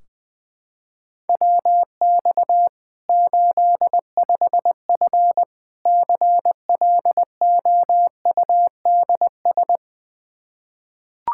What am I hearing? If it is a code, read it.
Morse code, heard as a keyed tone, about 21 wpm, 700 Hz. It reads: WX 85F CLOUDS
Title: Practice QSO - Exchange 2 - Info - 20wpm